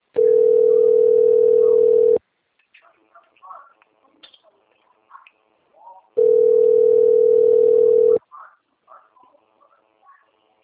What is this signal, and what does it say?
Ringback tone (North America), 440 + 480 Hz, with speech.
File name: rg-1600-unknown- - - .10231